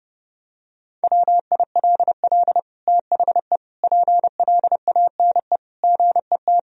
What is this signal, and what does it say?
Morse code, heard as a keyed tone, about 30 wpm, 700 Hz. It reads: WILL THE PLANE GET
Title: Sentences from Top 1000 Words - Review Part 3 - 30wpm